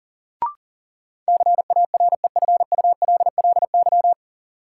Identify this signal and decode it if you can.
Morse code, heard as a keyed tone, about 40 wpm, 700 Hz. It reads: CAREFULLY
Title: Sentences from Top 1300 Words - New Word Drill Alt - 40wpm